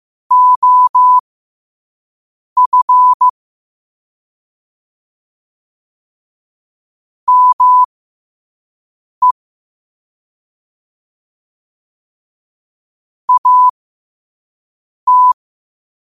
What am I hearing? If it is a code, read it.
Morse code, heard as a keyed tone, about 15 wpm, spaced as for 5 wpm, 1000 Hz. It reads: OF ME AT
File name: qotdfiles / 5wpm